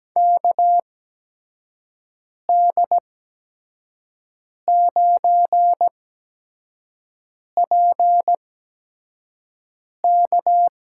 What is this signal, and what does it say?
Morse code, heard as a keyed tone, about 17 wpm, spaced as for 5 wpm, 700 Hz. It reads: KD9PK
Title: USA Call Signs - 5 Characters - 8x Character Spacing - 17wpm